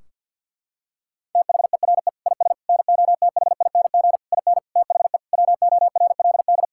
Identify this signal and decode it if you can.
Morse code, heard as a keyed tone, about 52 wpm, 700 Hz. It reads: THERE IS NOTHING IN THE WORLD
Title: Sentences from Top 1600 Words 50wpm